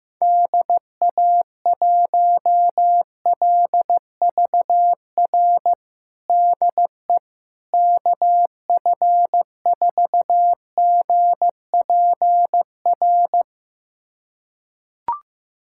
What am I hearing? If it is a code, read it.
Morse code, heard as a keyed tone, about 15 wpm, 700 Hz. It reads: DA1LVR DE KF4GPR
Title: USA Callsign DE Callsign - 6 Characters - 15wpm